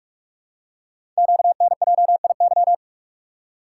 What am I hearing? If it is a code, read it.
Morse code, heard as a keyed tone, about 44 wpm, 700 Hz. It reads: QNJIY
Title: Code Groups - 5 Letters - 45wpm